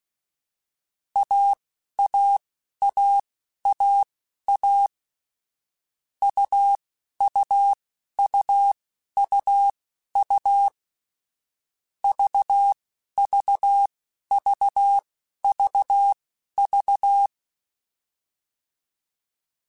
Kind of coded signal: Morse code